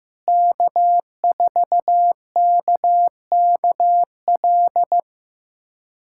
Morse code, heard as a keyed tone, about 16 wpm, 700 Hz. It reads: K4KKL
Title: USA Callsign DE Callsign - 5 Characters - 15wpm